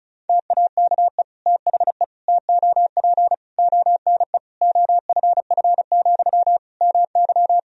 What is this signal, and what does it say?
Morse code, heard as a keyed tone, about 36 wpm, 700 Hz. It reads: TAKE THE TOP ONE OFF, MY
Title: Sentences from Top 1000 Words - Review Part 2 - 35wpm